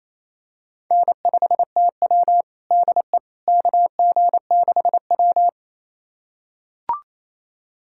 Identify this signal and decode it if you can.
Morse code, heard as a keyed tone, about 28 wpm, 700 Hz. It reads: N5TW DE KG6W
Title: USA Callsign DE Callsign - 4 Characters - 28wpm